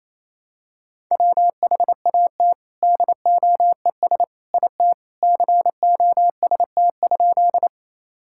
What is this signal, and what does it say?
Morse code, heard as a keyed tone, about 28 wpm, 700 Hz. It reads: WHAT DOES IT COST?
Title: Sentences from Top 1000 Words - Review Part 5 - 28wpm